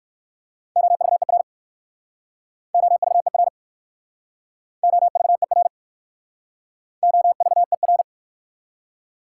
Morse code, heard as a keyed tone, about 58 wpm, 690 Hz. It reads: OVER OVER OVER OVER